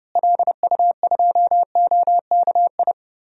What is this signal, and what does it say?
Morse code, heard as a keyed tone, about 31 wpm, 700 Hz. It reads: LU2OKI
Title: DXCC-100 Call Signs - 6 Characters - Repeated 3x - 30wpm